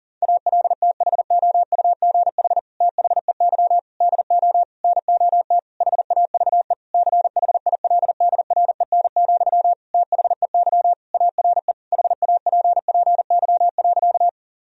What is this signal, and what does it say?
Morse code, heard as a keyed tone, about 37 wpm, 700 Hz. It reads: ALTHOUGH THEY DO NOT HAVE CHILDREN, THEY ARE HAPPY.
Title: Sentences from Top 1000 Words - Review Part 3 - 40wpm